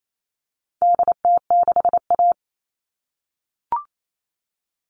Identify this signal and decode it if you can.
Morse code, heard as a keyed tone, about 28 wpm, 700 Hz. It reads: DT6A
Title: All Cut Numbers - 4-Digit 28wpm